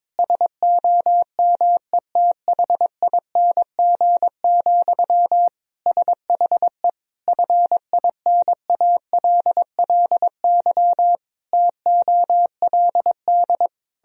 Morse code, heard as a keyed tone, about 22 wpm, 700 Hz. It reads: SOMETHING, SHE FINALLY TOLD
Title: Sentences from Top 1000 Words - Review Part 3 - 22wpm